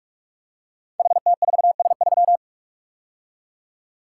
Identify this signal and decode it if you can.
Morse code, heard as a keyed tone, about 44 wpm, 700 Hz. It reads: HT4S3
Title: Code Groups - 5 Letter:Numbers - 45wpm